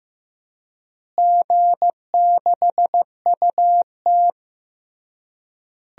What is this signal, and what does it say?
Morse code, heard as a keyed tone, about 15 wpm, 700 Hz. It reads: G6UT
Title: DXCC-100 Call Signs - 4 Characters - Repeated 3x - 15wpm